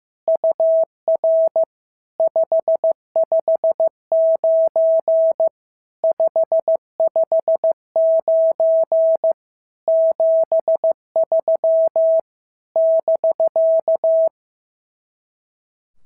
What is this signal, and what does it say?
Morse code, heard as a keyed tone, about 14 wpm, 650 Hz. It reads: UR 559 559 73 <BK>
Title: SOTA QSO Practice 15wpm